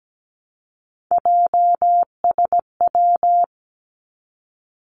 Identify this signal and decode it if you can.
Morse code, heard as a keyed tone, about 17 wpm, 700 Hz. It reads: JSW
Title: Three Challenging Letter-Number Combinations 17wpm